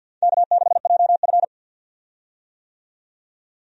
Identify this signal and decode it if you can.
Morse code, heard as a keyed tone, about 49 wpm, 700 Hz. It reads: K6JF
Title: USA Call Signs - 4 Characters - 50wpm